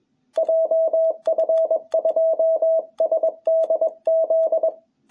Morse code, heard as a keyed tone, about 21 wpm, 660 Hz. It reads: JF2SDZ
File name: jh8cbh / 2021cw